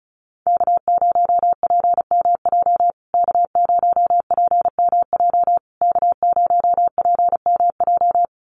Morse code, heard as a keyed tone, about 34 wpm, 700 Hz. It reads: K0PMJ K0PMJ K0PMJ
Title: USA Call Signs Repeated 3x 35wpm